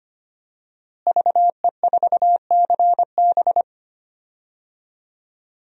Morse code, heard as a keyed tone, about 25 wpm, 700 Hz. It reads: VE4CB
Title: DXCC-100 Call Signs - 5 Characters - Repeated 3x - 25wpm